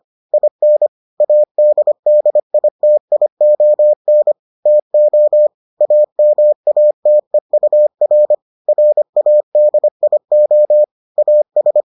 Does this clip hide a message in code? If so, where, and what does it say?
Yes: Morse code from 0.3 to 11.8 s, heard as a keyed tone, about 25 wpm, 600 Hz. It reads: IN ADDITION TO AMATEUR RADIO AS